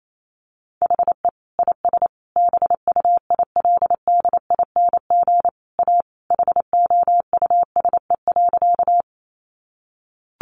Morse code, heard as a keyed tone, about 28 wpm, 700 Hz. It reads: HE IS BUILDING A HOUSE.
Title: Sentences from Top 700 Words 28wpm